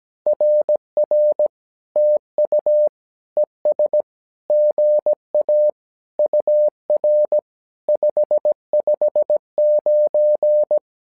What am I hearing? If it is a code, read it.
Morse code, heard as a keyed tone, about 17 wpm, 600 Hz. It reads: RR TU ES GA UR 559
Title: SOTA QSO Practice 17wpm